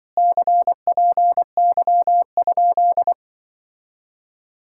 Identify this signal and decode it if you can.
Morse code, heard as a keyed tone, about 25 wpm, 700 Hz. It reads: CPY?
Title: Speed-Racing Common QSO Phrases - 20wpm